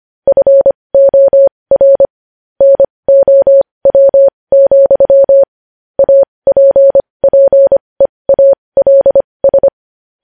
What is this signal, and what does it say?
Morse code, heard as a keyed tone, about 25 wpm, spaced as for 20 wpm, 550 Hz. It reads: FOR NOW, APPEALS
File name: podcast / News headlines-1x-25x20